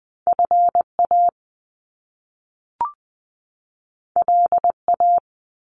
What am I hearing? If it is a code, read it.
Morse code, heard as a keyed tone, about 20 wpm, 700 Hz. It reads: FA LA
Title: Top Two Letter Combinations in Words 20wpm